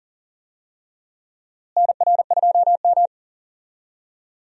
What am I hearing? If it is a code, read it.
Morse code, heard as a keyed tone, about 40 wpm, 700 Hz. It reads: NR2M
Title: USA Call Signs - 4 Characters - Repeated 3x - 40wpm